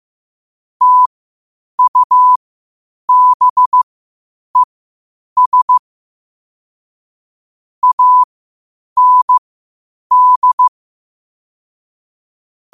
Morse code, heard as a keyed tone, about 15 wpm, spaced as for 8 wpm, 1000 Hz. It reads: TUBES AND
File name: qotdfiles / 8wpm